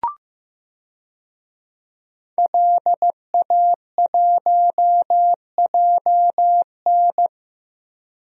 Morse code, heard as a keyed tone, about 15 wpm, 700 Hz. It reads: LA1JN
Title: DXCC-100 Call Signs - 5 Characters - 15wpm